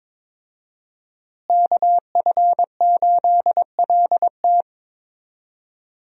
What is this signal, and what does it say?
Morse code, heard as a keyed tone, about 22 wpm, 700 Hz. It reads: KF8LT
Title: USA Call Signs - 5 Characters - Repeated 3x - 22wpm